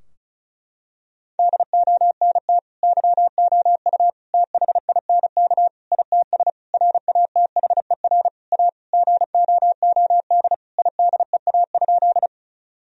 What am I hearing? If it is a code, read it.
Morse code, heard as a keyed tone, about 34 wpm, 700 Hz. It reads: DONT YOU THINK ITS RATHER A GOOD IDEA?
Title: Sentences from Top 1000 Words - Review Part 4 - 35wpm